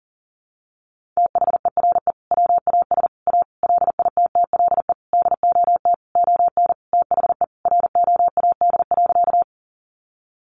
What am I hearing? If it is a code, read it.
Morse code, heard as a keyed tone, about 37 wpm, 700 Hz. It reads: THERE WAS A LITTLE DOT ON THE ROAD.
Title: Sentences from Top 1500 Words 40wpm